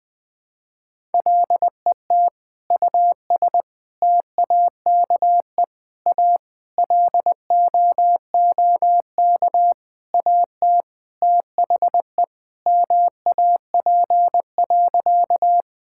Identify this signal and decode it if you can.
Morse code, heard as a keyed tone, about 20 wpm, 700 Hz. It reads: LET US TAKE A LOOK AT THE MAP.
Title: Sentences with 5-Word Phrases - 20wpm